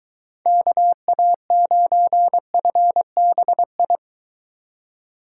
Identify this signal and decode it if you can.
Morse code, heard as a keyed tone, about 23 wpm, 700 Hz. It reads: KA9FBI